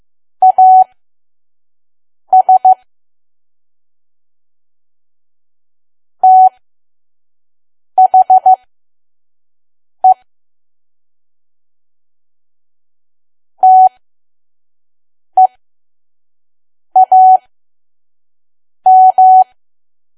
Morse code, heard as a keyed tone, about 15 wpm, spaced as for 5 wpm, 750 Hz. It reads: AS THE TEAM
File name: Archive / 5 WPM files / PM